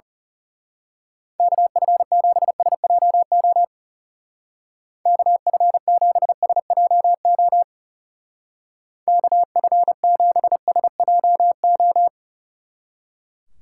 Morse code, heard as a keyed tone, about 34 wpm, 700 Hz. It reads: KF7SJO KF7SJO KF7SJO
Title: Speed Racing - USA 2x3 Call Signs 30wpm